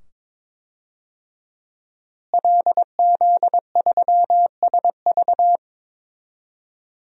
Morse code, heard as a keyed tone, about 22 wpm, 700 Hz. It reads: LZ3SV
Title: DXCC-100 Call Signs - 5 Characters - Repeated 2x - 22wpm